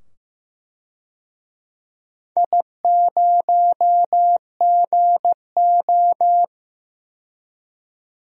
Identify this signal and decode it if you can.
Morse code, heard as a keyed tone, about 15 wpm, 700 Hz. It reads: I0GO
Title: DXCC-100 Call Signs - 4 Characters - 15wpm